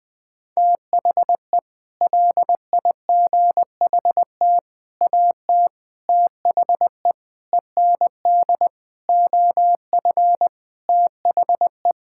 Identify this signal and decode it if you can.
Morse code, heard as a keyed tone, about 20 wpm, 700 Hz. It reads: THE LIGHT AT THE END OF THE
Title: Sets of 8 Words - 20wpm